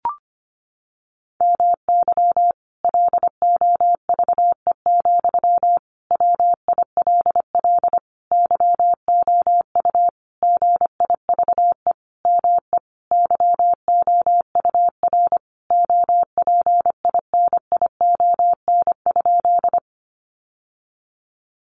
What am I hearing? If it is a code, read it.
Morse code, heard as a keyed tone, about 25 wpm, 700 Hz. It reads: MY LOVE, WILL YOU GIVE ME YOUR OPINION?
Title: Sentences from Top 1200 Words 25wpm